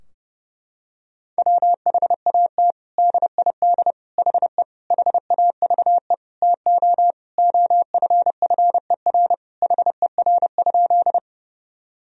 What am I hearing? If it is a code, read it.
Morse code, heard as a keyed tone, about 30 wpm, 700 Hz. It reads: WHAT DID HE HAVE TO OFFER HER?
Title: Sentences from Top 1000 Words - Review Part 4 - 30wpm